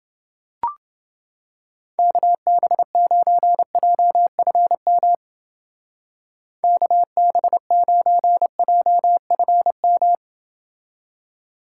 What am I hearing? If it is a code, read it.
Morse code, heard as a keyed tone, about 29 wpm, 700 Hz. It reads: KB9JFM KB9JFM